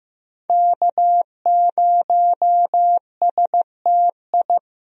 Morse code, heard as a keyed tone, about 15 wpm, 700 Hz. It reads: K0STI
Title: USA Call Signs - 5 Characters - Repeated 2x - 15wpm